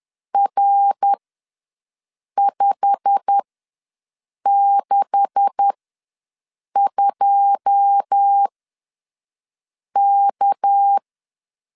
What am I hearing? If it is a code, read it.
Morse code, heard as a keyed tone, about 10 wpm, spaced as for 5 wpm, 800 Hz. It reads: R562K
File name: LEZIONI / 10 SIGNS / 10D COMA ws 10wpm 4314